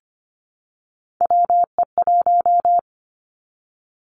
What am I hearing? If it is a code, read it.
Morse code, heard as a keyed tone, about 25 wpm, 700 Hz. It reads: WE1
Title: Three Challenging Letter-Number Combinations 25wpm